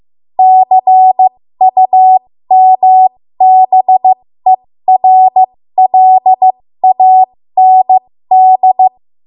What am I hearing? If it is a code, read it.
Morse code, heard as a keyed tone, about 15 wpm, spaced as for 13 wpm, 750 Hz. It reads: CUMBERLAND
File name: Morse / 13WPM